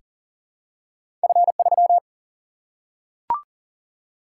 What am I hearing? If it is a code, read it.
Morse code, heard as a keyed tone, about 41 wpm, 700 Hz. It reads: F3
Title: Single-Letter Single-Number 40wpm